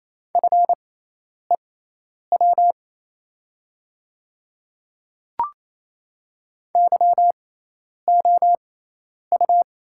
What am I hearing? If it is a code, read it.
Morse code, heard as a keyed tone, about 28 wpm, spaced as for 10 wpm, 690 Hz. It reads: FEW YOU
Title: 6x Character Spacing - Top 300 Words - 28wpm